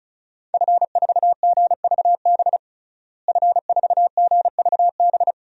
Morse code, heard as a keyed tone, about 34 wpm, 700 Hz. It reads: F4GVB F4GVB
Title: DXCC-100 Call Signs - 5 Characters - Repeated 3x - 35wpm